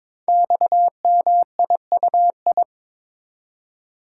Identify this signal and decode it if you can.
Morse code, heard as a keyed tone, about 22 wpm, 700 Hz. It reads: XMIUI
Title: Code Groups - 5 Letters - 22wpm